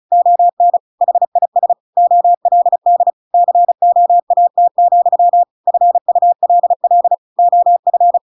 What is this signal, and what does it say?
Morse code, heard as a keyed tone, about 34 wpm, 700 Hz. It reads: ON HIS OLD COAT, FULL OF
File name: Pinocchio / Pinocchio ch8 35 35 ik1hge